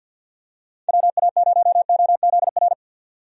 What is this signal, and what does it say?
Morse code, heard as a keyed tone, about 51 wpm, 700 Hz. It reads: WA0OZR
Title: USA Call Signs - 6 Characters - 50wpm